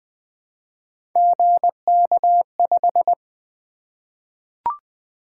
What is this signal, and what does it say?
Morse code, heard as a keyed tone, about 20 wpm, 700 Hz. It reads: GK5
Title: Three Letter-Number Combinations - 20wpm